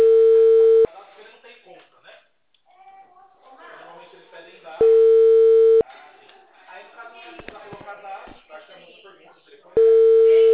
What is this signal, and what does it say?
Ringback tone (China, 450 Hz), with speech.